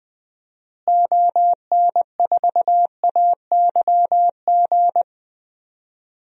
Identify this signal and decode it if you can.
Morse code, heard as a keyed tone, about 20 wpm, 700 Hz. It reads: ON4AYG
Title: DXCC-100 Call Signs - 6 Characters - Repeated 2x - 20wpm